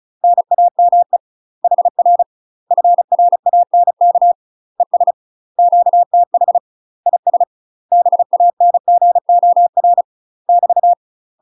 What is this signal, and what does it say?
Morse code, heard as a keyed tone, about 36 wpm, 700 Hz. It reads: NAME HR FRANK ES QTH IS BANGOR <BT>
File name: UK MASTER QSOs TRAINING / 35WPM / UK MASTER QSOs 0013